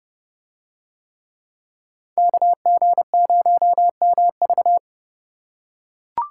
Morse code, heard as a keyed tone, about 28 wpm, 700 Hz. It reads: KG0MV